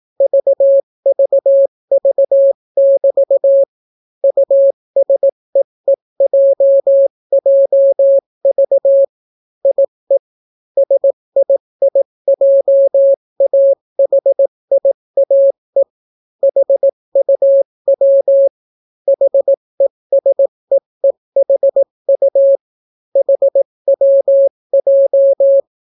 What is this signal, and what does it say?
Morse code, heard as a keyed tone, about 18 wpm, 550 Hz. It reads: VVV<BT> USEEJJV IE SIIJAHIAE HUW HESEEHU HWJ